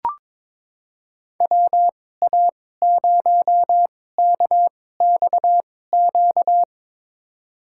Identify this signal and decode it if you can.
Morse code, heard as a keyed tone, about 22 wpm, spaced as for 16 wpm, 700 Hz. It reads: WA0KXQ
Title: USA Call Signs - 6 Characters - 2x Character Spacing - 22wpm